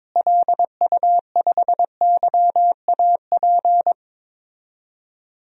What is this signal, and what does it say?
Morse code, heard as a keyed tone, about 22 wpm, 700 Hz. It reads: LU5YAP
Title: DXCC-100 Call Signs - 6 Characters - Repeated 3x - 22wpm